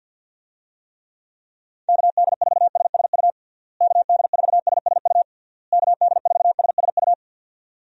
Morse code, heard as a keyed tone, about 50 wpm, 700 Hz. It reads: KD4SSU KD4SSU KD4SSU
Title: USA Call Signs - 6 Characters - Repeated 3x - 50wpm